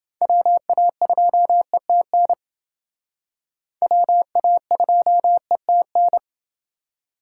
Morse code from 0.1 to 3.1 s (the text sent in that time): WA2ETN